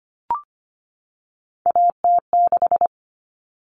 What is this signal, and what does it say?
Morse code, heard as a keyed tone, about 25 wpm, 700 Hz. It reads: AT6
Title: Common Cut Numbers - 3-Digit 25wpm